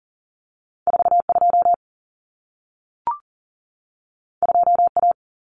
Morse code, heard as a keyed tone, about 40 wpm, 700 Hz. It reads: 42 2A
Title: Common Cut Numbers - 2-Digit 40wpm